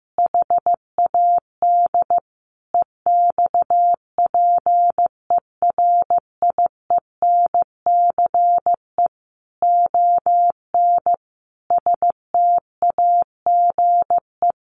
Morse code, heard as a keyed tone, about 15 wpm, 700 Hz. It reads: HAD EXPERIENCE ON STAGE
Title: Sentences from Top 1200 Words 15wpm